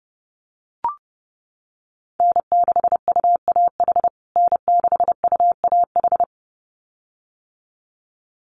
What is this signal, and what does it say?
Morse code, heard as a keyed tone, about 30 wpm, 700 Hz. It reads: N6UAH N6UAH
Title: USA Call Signs Repeated 2x 30wpm